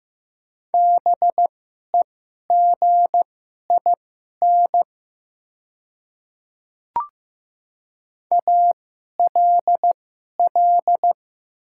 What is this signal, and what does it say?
Morse code, heard as a keyed tone, about 15 wpm, spaced as for 11 wpm, 700 Hz. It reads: BEGIN ALL
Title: 2x Character Spacing - Top 500 Words - 15wpm